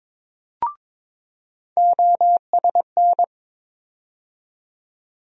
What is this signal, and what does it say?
Morse code, heard as a keyed tone, about 22 wpm, 700 Hz. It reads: OSN
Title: USA Call Sign Postfixes - 3 Letters 22wpm